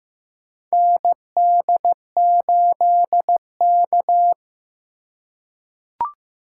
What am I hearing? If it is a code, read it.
Morse code, heard as a keyed tone, about 15 wpm, 700 Hz. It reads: ND8K
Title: USA Call Signs - 4x Character Spacing - 15wpm